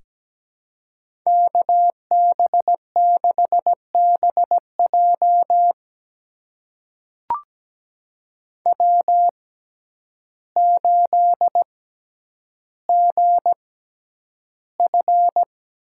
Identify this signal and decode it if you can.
Morse code, heard as a keyed tone, about 17 wpm, spaced as for 9 wpm, 700 Hz. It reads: KB6BJ W8GF
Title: USA Call Signs - 5 Characters - 6x Character Spacing - 17wpm